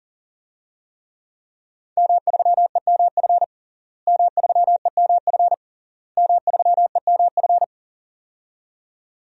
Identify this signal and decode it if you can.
Morse code, heard as a keyed tone, about 37 wpm, 700 Hz. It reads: M3EMF M3EMF M3EMF
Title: DXCC-100 Call Signs - 5 Characters - Repeated 3x - 40wpm